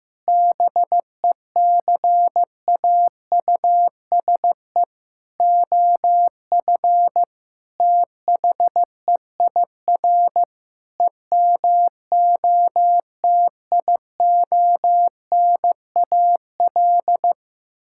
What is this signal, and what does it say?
Morse code, heard as a keyed tone, about 15 wpm, 700 Hz. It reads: BECAUSE OF THEIR EMOTIONAL
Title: Sentences from Top 1000 Words - Review Part 5 - 15wpm